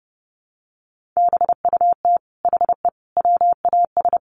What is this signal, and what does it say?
Morse code, heard as a keyed tone, about 30 wpm, 700 Hz. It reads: BUT HE WAS
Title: Sentences from Top 1500 Words 30wpm